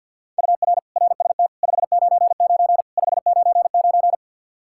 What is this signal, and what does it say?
Morse code, heard as a keyed tone, about 52 wpm, 700 Hz. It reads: UR RST 599 599
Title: Practice QSO - Exchange 1 - Beginning - 50wpm